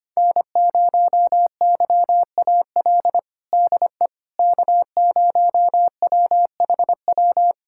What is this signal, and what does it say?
Morse code, heard as a keyed tone, about 25 wpm, 700 Hz. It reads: N0YAL DE K0WHW
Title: USA Callsign DE Callsign - 5 Characters - 25wpm